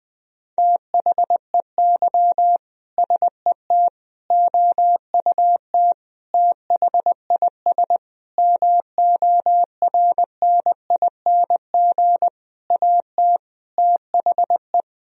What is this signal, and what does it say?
Morse code, heard as a keyed tone, about 20 wpm, 700 Hz. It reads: THEY SET OUT THIS MORNING AT THE
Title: Sentences from Top 1000 Words - Review Part 2 - 20wpm